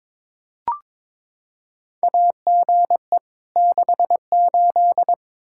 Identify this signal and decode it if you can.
Morse code, heard as a keyed tone, about 21 wpm, 700 Hz. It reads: AGE 68
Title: Practice QSO - Exchange 3 - More Info - 22wpm